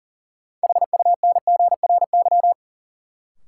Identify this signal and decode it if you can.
Morse code, heard as a keyed tone, about 40 wpm, 700 Hz. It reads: HUNGRY
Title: Sentences from Top 1400 Words - New Word Drill - 40wpm